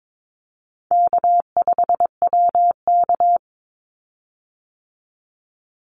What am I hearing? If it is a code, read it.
Morse code, heard as a keyed tone, about 22 wpm, 700 Hz. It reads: K5WK
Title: Easy Call Signs Repeated 2x - 22wpm